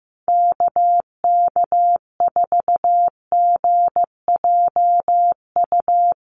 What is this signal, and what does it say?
Morse code, heard as a keyed tone, about 15 wpm, 700 Hz. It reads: KK4GJU
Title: USA Call Signs Repeated 3x 15wpm